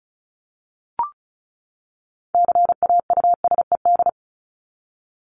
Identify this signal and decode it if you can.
Morse code, heard as a keyed tone, about 34 wpm, 700 Hz. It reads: CAUSED
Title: ED Suffix - Top 100 Words with Base Words 35wpm